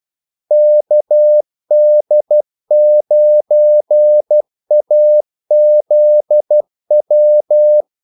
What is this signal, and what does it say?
Morse code, heard as a keyed tone, about 12 wpm, 600 Hz. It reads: KD9AZW
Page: 2 ekstra ordmellemrum mellem kaldesignalerne